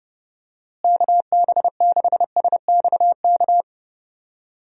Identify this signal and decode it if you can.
Morse code, heard as a keyed tone, about 30 wpm, 700 Hz. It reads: KB6SXK
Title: USA Call Signs - 6 Characters - Repeated 2x - 30wpm